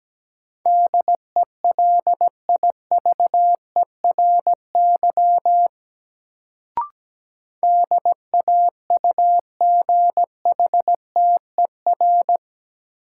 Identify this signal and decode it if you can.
Morse code, heard as a keyed tone, about 17 wpm, 700 Hz. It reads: DELIVERY DAUGHTER
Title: Sentences from Top 1500 Words - New Word Drill - 17wpm